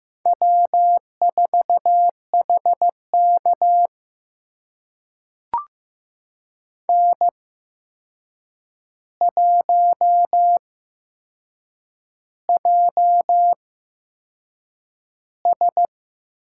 Morse code, heard as a keyed tone, about 15 wpm, spaced as for 6 wpm, 700 Hz. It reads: W4HK N1JS
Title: USA Call Signs - 8x Character Spacing - 15wpm